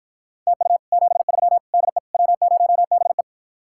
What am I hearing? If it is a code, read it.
Morse code, heard as a keyed tone, about 52 wpm, 700 Hz. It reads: TU 73 DE W0BE